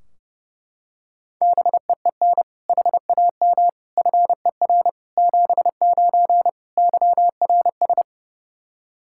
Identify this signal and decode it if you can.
Morse code, heard as a keyed tone, about 32 wpm, 700 Hz. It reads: BEEN HAM FER 79 YRS
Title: Practice QSO - Exchange 3 - More Info - 30wpm